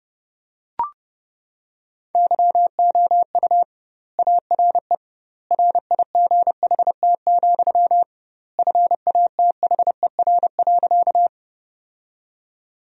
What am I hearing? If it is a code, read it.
Morse code, heard as a keyed tone, about 30 wpm, 700 Hz. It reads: YOU ARE RIGHT, FATHER.
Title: 2x Word-Spacing - Sentences from Top 300 Words - 30wpm